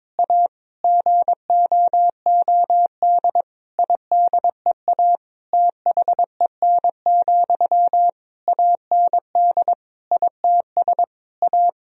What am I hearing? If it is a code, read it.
Morse code, heard as a keyed tone, about 21 wpm, 700 Hz. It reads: A GOOD IDEA THEN, AND ITS A
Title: Sentences from Top 1000 Words - Review Part 4 - 22wpm